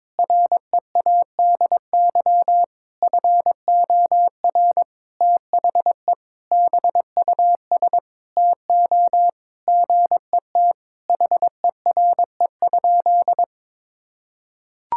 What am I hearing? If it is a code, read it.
Morse code, heard as a keyed tone, about 22 wpm, 700 Hz. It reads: READY FOR THE BUS TO GET HERE?
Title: Sentences from Top 1000 Words - Review Part 6 - 22wpm